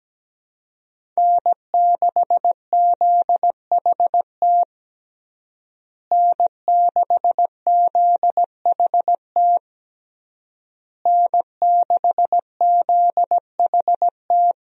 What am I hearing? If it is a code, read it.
Morse code, heard as a keyed tone, about 17 wpm, 700 Hz. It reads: N6ZHT N6ZHT N6ZHT
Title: USA Call Signs - 5 Characters - Repeated 3x - 17wpm